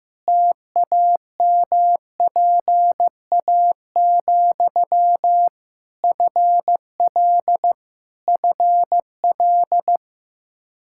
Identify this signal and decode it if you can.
Morse code, heard as a keyed tone, about 15 wpm, 700 Hz. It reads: TAMPA, FL FL
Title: Practice QSO - Exchange 1 - Beginning - 15wpm